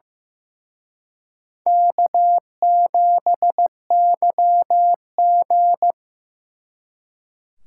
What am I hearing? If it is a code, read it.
Morse code, heard as a keyed tone, about 18 wpm, 700 Hz. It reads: K7YG
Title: Speed Racing - USA Call Signs 15wpm